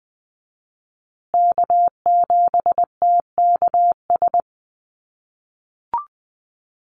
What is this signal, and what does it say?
Morse code, heard as a keyed tone, about 20 wpm, 700 Hz. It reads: K7TKS
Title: USA Call Signs Repeated 3x 20wpm